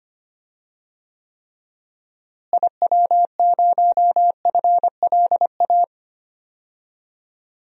Morse code, heard as a keyed tone, about 25 wpm, 700 Hz. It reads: IW0FLA
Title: DXCC-100 Call Signs - 6 Characters - Repeated 3x - 25wpm